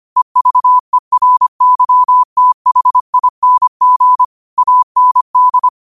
Morse code, heard as a keyed tone, about 25 wpm, 1000 Hz. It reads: EVERYTHING AND